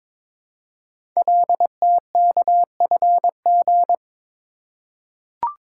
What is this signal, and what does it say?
Morse code, heard as a keyed tone, about 22 wpm, 700 Hz. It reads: LTKFG
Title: Code Groups - 5 Letters - 22wpm